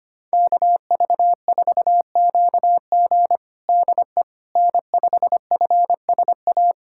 Morse code, heard as a keyed tone, about 25 wpm, 700 Hz. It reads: KV4QG DE N5FSA
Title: USA Callsign DE Callsign - 5 Characters - 25wpm